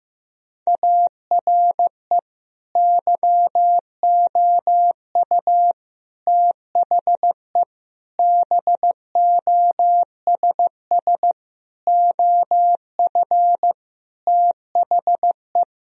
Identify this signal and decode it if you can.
Morse code, heard as a keyed tone, about 16 wpm, 700 Hz. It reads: ARE YOU THE BOSS OF THE
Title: Sentences from Top 1000 Words - Review Part 4 - 15wpm